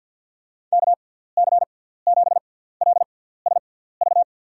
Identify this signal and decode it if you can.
Morse code, heard as a keyed tone, about 48 wpm, spaced as for 17 wpm, 700 Hz. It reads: KC7LSV
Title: USA Call Signs - 6 Characters - 6x Character Spacing - 50wpm